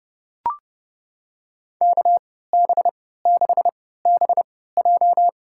Morse code, heard as a keyed tone, about 30 wpm, spaced as for 17 wpm, 700 Hz. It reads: KB6BJ
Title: USA Call Signs - 5 Characters - 3x Character Spacing - 30wpm